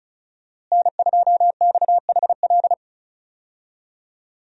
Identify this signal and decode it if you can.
Morse code, heard as a keyed tone, about 36 wpm, 700 Hz. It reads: N2XHL
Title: USA Call Signs - 5 Characters - Repeated 2x - 35wpm